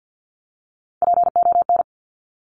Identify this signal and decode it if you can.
Morse code, heard as a keyed tone, about 50 wpm, 700 Hz. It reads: PON
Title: Top Three Letter Combinations in Words 50wpm